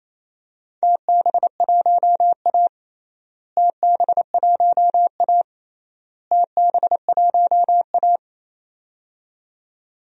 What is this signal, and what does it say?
Morse code, heard as a keyed tone, about 28 wpm, 700 Hz. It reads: TB1A TB1A TB1A
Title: DXCC-100 Call Signs - 4 Characters - Repeated 3x - 28wpm